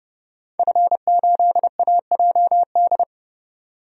Morse code, heard as a keyed tone, about 32 wpm, 700 Hz. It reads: F8AJD
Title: DXCC-100 Call Signs - 5 Characters - 30wpm